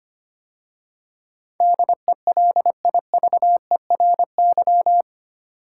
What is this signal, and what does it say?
Morse code, heard as a keyed tone, about 25 wpm, 700 Hz. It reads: DELIVERY
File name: 8 Letter Words - First 3 Letters Given - 25wpm